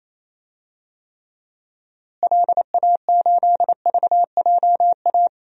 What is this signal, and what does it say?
Morse code, heard as a keyed tone, about 28 wpm, 700 Hz. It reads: LA8VJA
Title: DXCC-100 Call Signs - 6 Characters - Repeated 3x - 28wpm